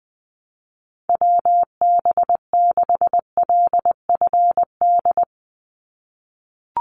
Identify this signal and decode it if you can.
Morse code, heard as a keyed tone, about 20 wpm, 700 Hz. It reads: WB6LFD
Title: USA Call Signs 20wpm